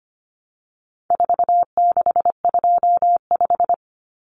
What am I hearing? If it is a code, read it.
Morse code, heard as a keyed tone, about 25 wpm, 700 Hz. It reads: 4625
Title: Common Cut Numbers - 4-Digit 25wpm